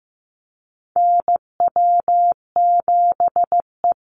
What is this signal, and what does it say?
Morse code, heard as a keyed tone, about 15 wpm, 700 Hz. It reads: NW7E
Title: NA SOTA Chaser Callsigns 15wpm